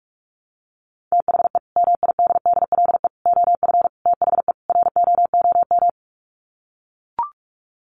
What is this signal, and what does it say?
Morse code, heard as a keyed tone, about 44 wpm, 700 Hz. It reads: THE MIDDLE OF THE ROOM
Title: Sets of 5 Words Encore2 45wpm